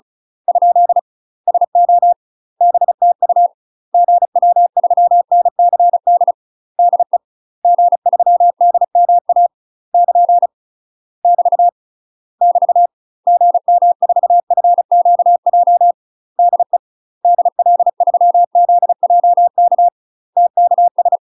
Morse code, heard as a keyed tone, about 36 wpm, 700 Hz. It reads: ? SO BTU GW3NCD DE G3DMA <KN> <BT> <BT> GM4FQJ DE DL3ZJK TKS
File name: UK MASTER QSOs TRAINING / 35WPM / UK MASTER QSOs 0016